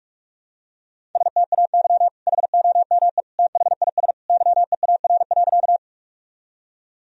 Morse code, heard as a keyed tone, about 42 wpm, 700 Hz. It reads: STAY HOME THIS YEAR.
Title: Sentences from Top 1000 Words - Review Part 1 - 45wpm